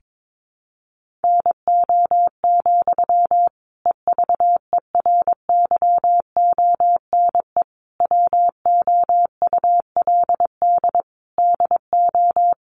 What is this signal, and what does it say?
Morse code, heard as a keyed tone, about 22 wpm, 700 Hz. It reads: NO, EVERYONE WOULD DO
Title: Sentences from Top 800 Words 22wpm